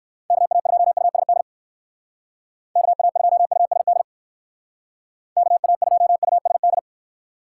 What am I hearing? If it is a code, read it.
Morse code, heard as a keyed tone, about 58 wpm, 700 Hz. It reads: KA2USD KA2USD KA2USD